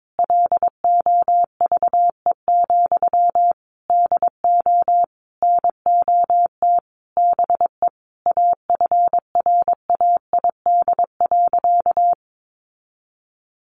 Morse code, heard as a keyed tone, about 22 wpm, 700 Hz. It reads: LOVE, DO NOT BE AFRAID.
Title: Sentences from Top 1500 Words 22wpm